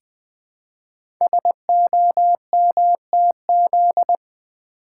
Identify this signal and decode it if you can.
Morse code, heard as a keyed tone, about 20 wpm, 700 Hz. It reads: SOMTZ
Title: Code Groups - 5 Letter:Numbers - 20wpm